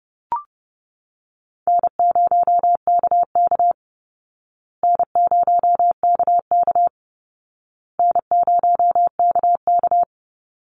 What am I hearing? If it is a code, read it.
Morse code, heard as a keyed tone, about 30 wpm, 700 Hz. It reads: N0KK N0KK N0KK
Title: Easy Call Signs Repeated 3x - 30wpm